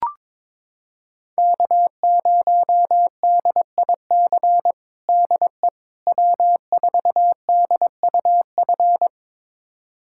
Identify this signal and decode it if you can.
Morse code, heard as a keyed tone, about 22 wpm, 700 Hz. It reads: K0DIC DE W4DUF
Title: USA Callsign DE Callsign - 5 Characters - 22wpm